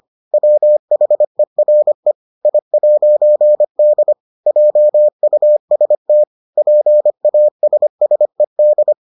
Morse code, heard as a keyed tone, about 25 wpm, 600 Hz. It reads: WHERE I'D JUST PASSED